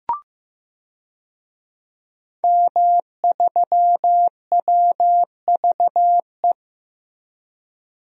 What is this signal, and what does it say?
Morse code, heard as a keyed tone, about 15 wpm, 700 Hz. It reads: M3WVE
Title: DXCC-100 Call Signs - 5 Characters - Repeated 3x - 15wpm